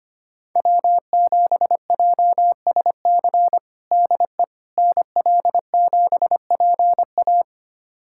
Morse code, heard as a keyed tone, about 25 wpm, 700 Hz. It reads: W7JSC DE NL7PA
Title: USA Callsign DE Callsign - 5 Characters - 25wpm